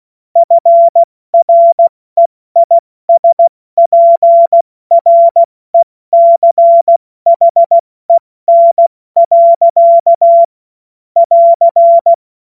Morse code, heard as a keyed tone, about 16 wpm, 680 Hz. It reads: FREISPRECHEN. <AR>
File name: Morsekurs / Witze / Witz 29. August 2023 70 F80